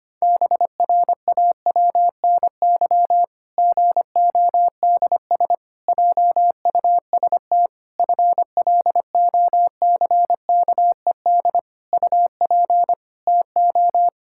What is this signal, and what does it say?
Morse code, heard as a keyed tone, about 25 wpm, 700 Hz. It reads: BRAWNY GODS JUST FLOCKED UP TO
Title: Pangram Challenge v2 - 25wpm